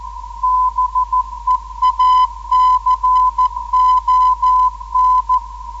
Morse code, heard as a keyed tone, about 14 wpm, 1010 Hz. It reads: BEACON
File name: hf beacon / zvok / ok0ef 10134khz